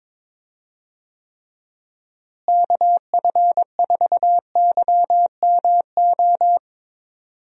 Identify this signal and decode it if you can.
Morse code, heard as a keyed tone, about 22 wpm, 700 Hz. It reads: KF4YMO
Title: USA Call Signs - 6 Characters - Repeated 3x - 22wpm